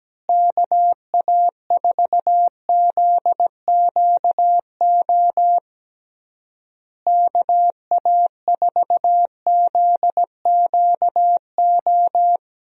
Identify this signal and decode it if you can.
Morse code, heard as a keyed tone, about 17 wpm, 700 Hz. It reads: KA4ZQO KA4ZQO
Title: USA Call Signs - 6 Characters - Repeated 3x - 17wpm